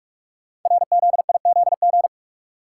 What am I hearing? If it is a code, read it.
Morse code, heard as a keyed tone, about 44 wpm, 700 Hz. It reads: RZIZG
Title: Code Groups - 5 Letters - 45wpm